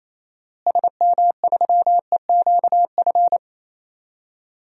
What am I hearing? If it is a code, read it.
Morse code, heard as a keyed tone, about 28 wpm, 700 Hz. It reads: SM3EQF
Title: DXCC-100 Call Signs - 6 Characters - Repeated 3x - 28wpm